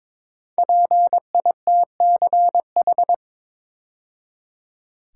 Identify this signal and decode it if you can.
Morse code, heard as a keyed tone, about 22 wpm, 700 Hz. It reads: PITCH
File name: 5 Letter Words - First 3 Letters Given - 22wpm